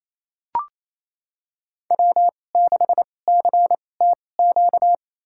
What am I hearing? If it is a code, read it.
Morse code, heard as a keyed tone, about 28 wpm, spaced as for 20 wpm, 700 Hz. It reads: W6CTQ
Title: USA Call Signs - 5 Characters - 2x Character Spacing - 28wpm